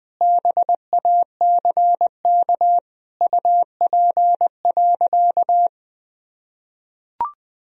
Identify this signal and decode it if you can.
Morse code, heard as a keyed tone, about 21 wpm, 700 Hz. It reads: BACK UP.
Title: Sentences from Top 1000 Words - Review Part 2 - 20wpm